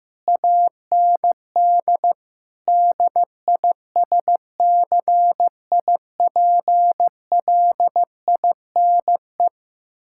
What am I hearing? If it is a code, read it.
Morse code, heard as a keyed tone, about 15 wpm, 700 Hz. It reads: AND DISCIPLINE
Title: Sentences from Top 1000 Words - Review Part 6 - 15wpm